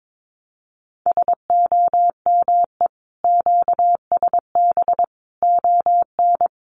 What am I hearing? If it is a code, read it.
Morse code, heard as a keyed tone, about 22 wpm, 700 Hz. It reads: SOME QSB ON
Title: Common QSO Phrases 22wpm